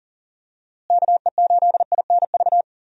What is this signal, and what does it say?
Morse code, heard as a keyed tone, about 40 wpm, 700 Hz. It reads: KE8INV
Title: USA Call Signs - 6 Characters - Repeated 3x - 40wpm